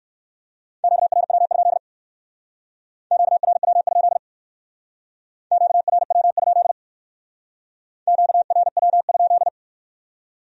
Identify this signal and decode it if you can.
Morse code, heard as a keyed tone, about 57 wpm, 700 Hz. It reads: QRW? QRW? QRW? QRW?